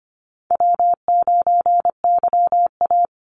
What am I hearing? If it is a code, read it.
Morse code, heard as a keyed tone, about 25 wpm, 700 Hz. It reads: W9YA
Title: Easy Call Signs - 25wpm